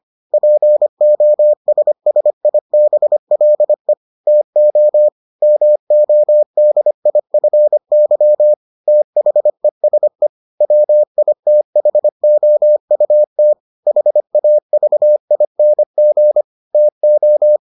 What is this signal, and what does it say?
Morse code, heard as a keyed tone, about 25 wpm, 600 Hz. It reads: POSSIBLE TO MODIFY THESE WITHOUT HAVING TO